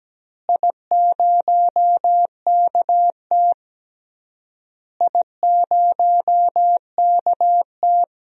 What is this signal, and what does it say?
Morse code, heard as a keyed tone, about 17 wpm, 700 Hz. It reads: I0KT I0KT
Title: DXCC-100 Call Signs - 4 Characters - Repeated 3x - 17wpm